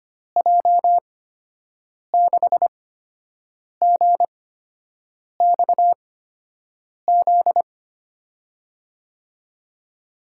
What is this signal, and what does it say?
Morse code, heard as a keyed tone, about 25 wpm, spaced as for 7 wpm, 700 Hz. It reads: J6GXZ
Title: USA Call Signs - 6 Characters - 8x Character Spacing - 25wpm